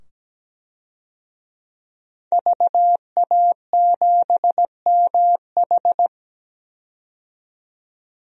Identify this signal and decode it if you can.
Morse code, heard as a keyed tone, about 17 wpm, 700 Hz. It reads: VA7MH
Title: DXCC-100 Call Signs - 5 Characters - 17wpm